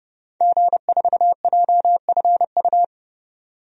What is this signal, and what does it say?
Morse code, heard as a keyed tone, about 30 wpm, 690 Hz. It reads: G4JFU
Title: DXCC-100 Call Signs - 5 Characters - 30wpm